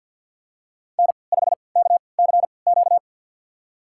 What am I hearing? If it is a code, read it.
Morse code, heard as a keyed tone, about 50 wpm, spaced as for 28 wpm, 700 Hz. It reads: N5KCQ
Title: USA Call Signs - 5 Characters - 3x Character Spacing - 50wpm